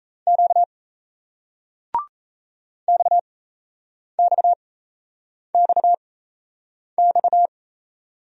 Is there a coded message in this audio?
Yes: Morse code from 0.3 to 7.5 s, heard as a keyed tone, about 36 wpm, spaced as for 8 wpm, 700 Hz. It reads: Q XXXX